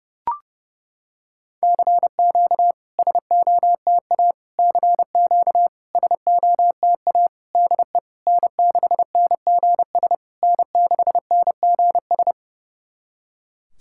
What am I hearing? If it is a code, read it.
Morse code, heard as a keyed tone, about 30 wpm, 700 Hz. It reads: CQ SOTA CQ SOTA DE N6NGS N6NGS
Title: SOTA QSO Practice 30wpm